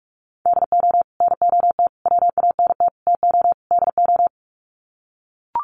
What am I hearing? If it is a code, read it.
Morse code, heard as a keyed tone, about 45 wpm, 700 Hz. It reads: DO NOT WANT TO DO
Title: Sets of 5 Words 45wpm